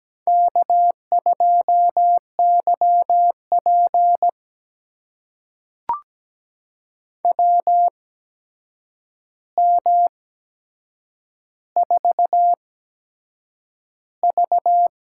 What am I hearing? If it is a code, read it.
Morse code, heard as a keyed tone, about 17 wpm, spaced as for 7 wpm, 700 Hz. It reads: K2YP WM4V
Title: USA Call Signs - 4 Characters - 8x Character Spacing - 17wpm